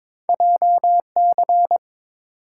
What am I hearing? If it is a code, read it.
Morse code, heard as a keyed tone, about 22 wpm, 700 Hz. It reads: JC
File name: practice / 22wpm / Two Letter-Number Combinations - 22wpm